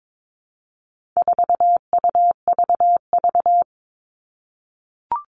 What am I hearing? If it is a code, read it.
Morse code, heard as a keyed tone, about 22 wpm, 700 Hz. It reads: 4UVV
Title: All Cut Numbers - 4-Digit 22wpm